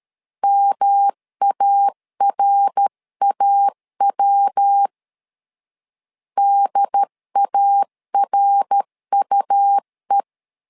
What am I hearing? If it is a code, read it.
Morse code, heard as a keyed tone, about 13 wpm, 800 Hz. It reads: MARAW DARUE
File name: LEZIONI / 3 MWRO / 3 MWR 12wpm 1711